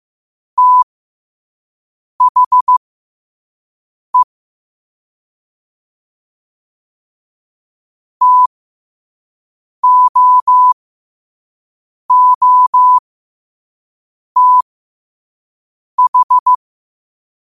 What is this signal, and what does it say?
Morse code, heard as a keyed tone, about 15 wpm, spaced as for 5 wpm, 1000 Hz. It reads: THE TOOTH
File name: qotdfiles / 5wpm